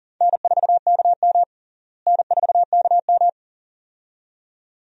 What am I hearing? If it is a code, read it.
Morse code, heard as a keyed tone, about 40 wpm, 700 Hz. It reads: N4KM N4KM
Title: USA Call Signs - 4 Characters - Repeated 2x - 40wpm